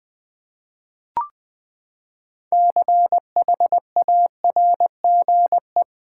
Morse code, heard as a keyed tone, about 21 wpm, 700 Hz. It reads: CHARGE
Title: Six Letter Words - 20wpm